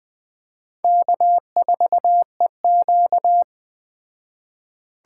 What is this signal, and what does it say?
Morse code, heard as a keyed tone, about 20 wpm, 700 Hz. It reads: K4EQ
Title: Speed Racing - USA Call Signs 17wpm